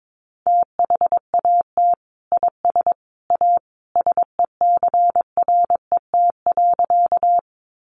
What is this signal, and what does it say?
Morse code, heard as a keyed tone, about 22 wpm, 700 Hz. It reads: THAT IS A SECRET.
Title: Sentences from Top 1200 Words 22wpm